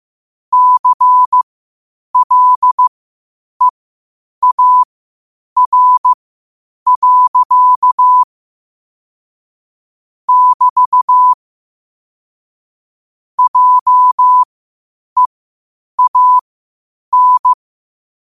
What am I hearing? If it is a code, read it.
Morse code, heard as a keyed tone, about 15 wpm, spaced as for 8 wpm, 1000 Hz. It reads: CLEAR. <BT> JEAN
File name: qotdfiles / 8wpm